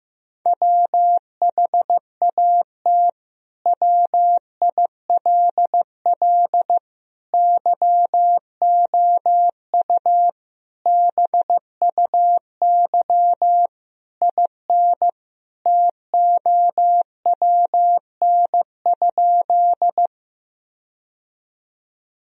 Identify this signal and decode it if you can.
Morse code, heard as a keyed tone, about 15 wpm, 700 Hz. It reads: WHAT WILL YOU BUY IN TOWN?
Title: Sentences from Top 1000 Words - Review Part 4 - 15wpm